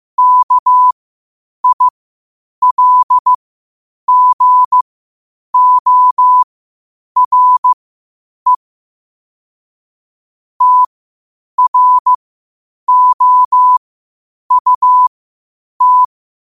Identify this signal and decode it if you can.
Morse code, heard as a keyed tone, about 15 wpm, spaced as for 8 wpm, 1000 Hz. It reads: KILGORE TROUT
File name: qotdfiles / 8wpm